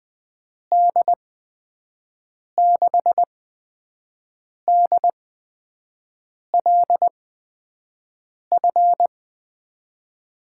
Morse code, heard as a keyed tone, about 20 wpm, spaced as for 5 wpm, 700 Hz. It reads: D6DLF
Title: USA Call Signs - 6 Characters - 8x Character Spacing - 20wpm